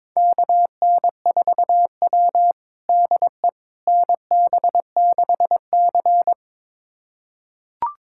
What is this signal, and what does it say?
Morse code, heard as a keyed tone, about 22 wpm, 700 Hz. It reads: KN4W DE NB6C
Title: USA Callsign DE Callsign - 4 Characters - 22wpm